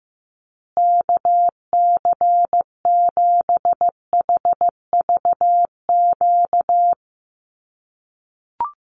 Morse code, heard as a keyed tone, about 15 wpm, 700 Hz. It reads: KC7HVQ
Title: USA Call Signs Repeated 3x 15wpm